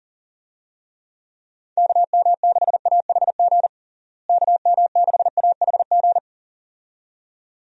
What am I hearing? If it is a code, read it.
Morse code, heard as a keyed tone, about 40 wpm, 700 Hz. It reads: KM6AHG KM6AHG
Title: USA Call Signs - 6 Characters - Repeated 2x - 40wpm